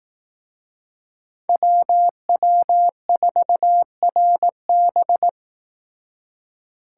Morse code, heard as a keyed tone, about 18 wpm, 700 Hz. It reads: WW4RB
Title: Speed Racing - USA 2x2 Call Signs 15wpm